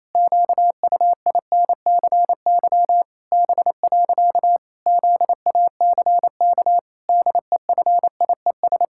Morse code, heard as a keyed tone, about 28 wpm, 700 Hz. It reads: QUINCY B. ZACK DEFIES
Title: Pangram Challenge v2 - 28wpm